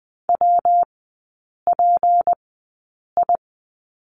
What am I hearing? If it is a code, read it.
Morse code, heard as a keyed tone, about 20 wpm, spaced as for 8 wpm, 690 Hz. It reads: WPI
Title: 159 Lesson 26e - Letter g - Three Characters 20wpm